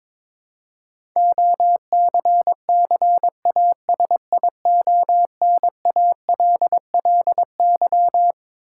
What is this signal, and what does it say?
Morse code, heard as a keyed tone, about 22 wpm, 700 Hz. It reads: OCCASIONALLY
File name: Twelve Letter Words - 22wpm